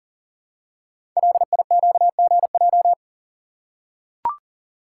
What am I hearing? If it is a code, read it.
Morse code, heard as a keyed tone, about 39 wpm, 700 Hz. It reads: LIQGJ
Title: Code Groups - 5 Letters - 40wpm